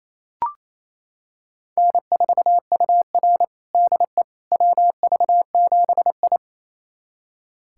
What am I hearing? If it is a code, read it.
Morse code, heard as a keyed tone, about 28 wpm, 700 Hz. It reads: N4UR DE WV7I
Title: USA Callsign DE Callsign - 4 Characters - 28wpm